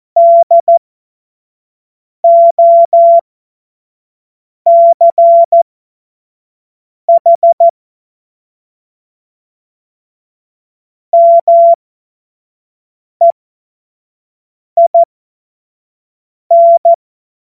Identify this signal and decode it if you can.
Morse code, heard as a keyed tone, about 14 wpm, spaced as for 5 wpm, 680 Hz. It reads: DOCH MEIN
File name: Morsekurs / Witze / Witz 10. Mai 2022 25 F70